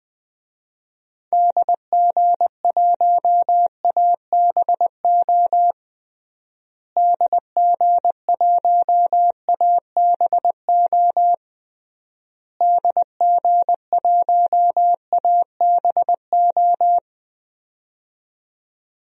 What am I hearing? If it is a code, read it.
Morse code, heard as a keyed tone, about 20 wpm, 700 Hz. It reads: DG1ABO DG1ABO DG1ABO
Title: DXCC-100 Call Signs - 6 Characters - Repeated 3x - 20wpm